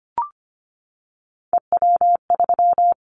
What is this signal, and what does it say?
Morse code, heard as a keyed tone, about 25 wpm, 700 Hz. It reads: EW3
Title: Callsign Prefixes for Top 50 DXCC - 25wpm